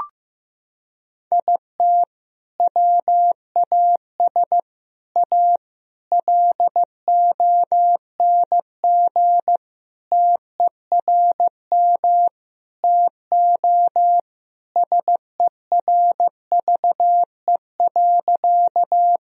Morse code, heard as a keyed tone, about 15 wpm, 700 Hz. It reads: IT WAS A LONG TERM TO SERVE.
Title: Sentences from Top 1000 Words - Review Part 4 - 15wpm